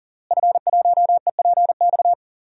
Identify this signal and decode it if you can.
Morse code, heard as a keyed tone, about 40 wpm, 700 Hz. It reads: F1EPX
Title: DXCC-100 Call Signs - 5 Characters - Repeated 2x - 40wpm